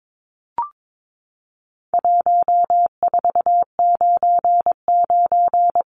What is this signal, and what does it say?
Morse code, heard as a keyed tone, about 22 wpm, 700 Hz. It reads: 1499
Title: Numbers - 4-Digit 22wpm